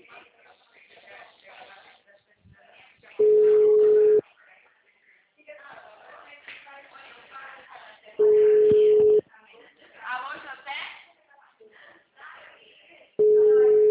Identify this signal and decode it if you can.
Ringback tone (Europe, 425 Hz), with speech.